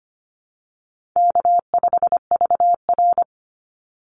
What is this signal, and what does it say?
Morse code, heard as a keyed tone, about 25 wpm, 700 Hz. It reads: K5VR
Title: Easy Call Signs Repeated 3x - 25wpm